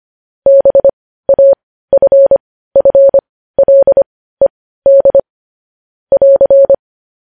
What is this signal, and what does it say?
Morse code, heard as a keyed tone, about 25 wpm, spaced as for 15 wpm, 550 Hz. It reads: BAFFLED <AR>
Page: news headlines in morse code at 15 wpm